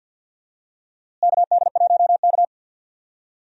Morse code, heard as a keyed tone, about 51 wpm, 700 Hz. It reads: KD1K
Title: USA Call Signs - 4 Characters - Repeated 3x - 50wpm